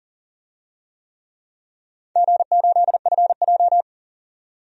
Morse code, heard as a keyed tone, about 40 wpm, 700 Hz. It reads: G8FJ
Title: DXCC-100 Call Signs - 4 Characters - 40wpm